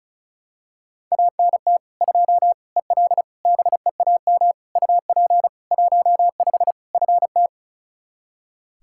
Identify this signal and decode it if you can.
Morse code, heard as a keyed tone, about 36 wpm, 700 Hz. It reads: ANT 2 EL BEAM UP 15 FT
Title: Practice QSO - Exchange 2 - Info - 35wpm